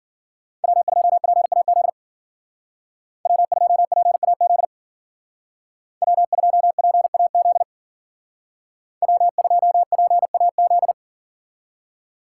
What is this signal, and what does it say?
Morse code, heard as a keyed tone, about 53 wpm, 700 Hz. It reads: W2PAZ W2PAZ W2PAZ W2PAZ